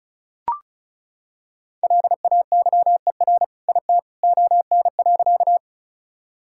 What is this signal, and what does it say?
Morse code, heard as a keyed tone, about 36 wpm, 700 Hz. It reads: LAYER IT ON.
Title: Sentences from Top 1600 Words 35wpm